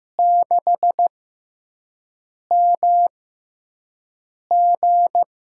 Morse code, heard as a keyed tone, about 15 wpm, spaced as for 5 wpm, 700 Hz. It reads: 6MG
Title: USA Call Signs - 5 Characters - 6x Character Spacing - 15wpm